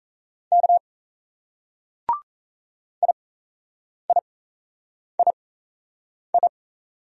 Morse code, heard as a keyed tone, about 35 wpm, spaced as for 8 wpm, 700 Hz. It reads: K IIII
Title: Speed-Racing Single Letters 28wpm